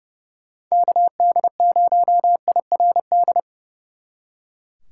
Morse code, heard as a keyed tone, about 40 wpm, 700 Hz. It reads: KD0IRD